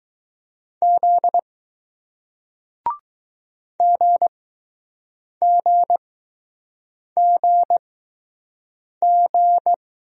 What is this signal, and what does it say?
Morse code, heard as a keyed tone, about 20 wpm, spaced as for 6 wpm, 700 Hz. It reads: Z GGGG